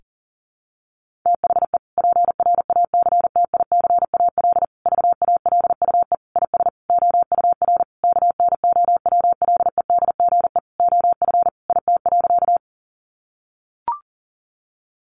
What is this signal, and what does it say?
Morse code, heard as a keyed tone, about 37 wpm, 700 Hz. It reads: THE PRACTICAL VALUE IS OUR KNOWLEDGE OF IT.
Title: Sentences from Top 1400 Words 40wpm